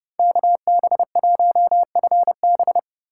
Morse code, heard as a keyed tone, about 32 wpm, 700 Hz. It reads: KB1FB
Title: USA Call Signs - 5 Characters - 3x Character Spacing - 30wpm